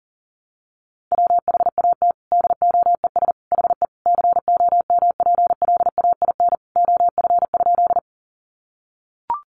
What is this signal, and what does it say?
Morse code, heard as a keyed tone, about 40 wpm, 700 Hz. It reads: WHAT DOES HE COMPLAIN OF?
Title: Sentences from Top 1300 Words 40wpm